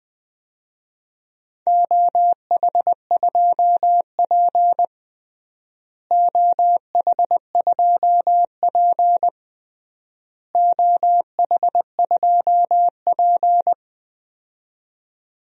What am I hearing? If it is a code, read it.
Morse code, heard as a keyed tone, about 20 wpm, 700 Hz. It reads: OH2P OH2P OH2P
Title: DXCC-100 Call Signs - 4 Characters - Repeated 3x - 20wpm